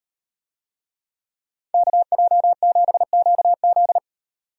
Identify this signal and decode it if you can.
Morse code, heard as a keyed tone, about 37 wpm, 700 Hz. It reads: KJ7QZ